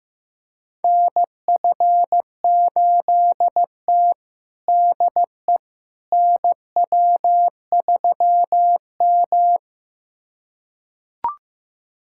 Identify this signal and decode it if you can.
Morse code, heard as a keyed tone, about 15 wpm, 700 Hz. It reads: NF8T DE NW3M
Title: USA Callsign DE Callsign - 4 Characters - 15wpm